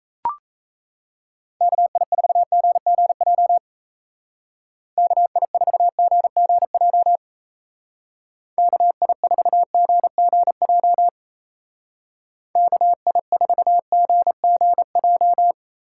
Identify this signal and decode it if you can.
Morse code, heard as a keyed tone, about 34 wpm, 700 Hz. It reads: KI4GGJ KI4GGJ KI4GGJ KI4GGJ
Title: Speed Racing - USA Call Signs 28wpm